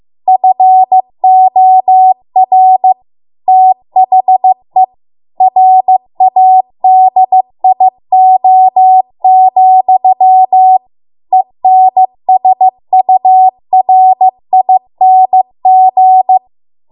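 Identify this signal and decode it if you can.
Morse code, heard as a keyed tone, about 15 wpm, 750 Hz. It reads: FOR THE RADIO, ENSURING
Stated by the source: Morse code practice at 15 wpm from the national amateur radio association's code archive.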